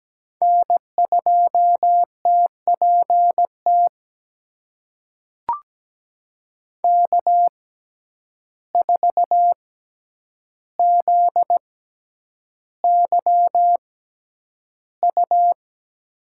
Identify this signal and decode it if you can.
Morse code, heard as a keyed tone, about 17 wpm, spaced as for 9 wpm, 700 Hz. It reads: N2TPT K4ZYU
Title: USA Call Signs - 5 Characters - 6x Character Spacing - 17wpm